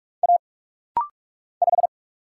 Morse code, heard as a keyed tone, about 44 wpm, 700 Hz. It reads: A 5
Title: Single Letter-Number Alt - Rapid-Fire 45wpm